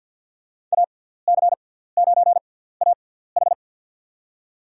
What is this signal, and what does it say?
Morse code, heard as a keyed tone, about 47 wpm, spaced as for 17 wpm, 700 Hz. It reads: AC9AH
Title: USA Call Signs - 6x Character Spacing - 50wpm